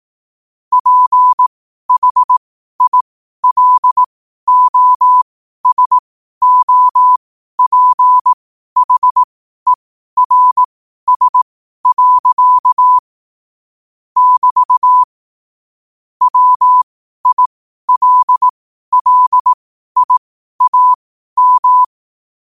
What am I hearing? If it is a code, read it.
Morse code, heard as a keyed tone, about 18 wpm, spaced as for 12 wpm, 1000 Hz. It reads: PHILOSOPHERS. <BT> WILLIAM
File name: qotdfiles / 12wpm